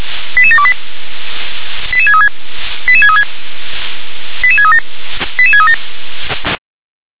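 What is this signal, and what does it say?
Five-tone selective call (ZVEI), heard as quick runs of five tones, tones of about 80 ms. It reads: 88728 88636 88636 88637 88637
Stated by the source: ZVEI five-tone call 88783 received via SDR.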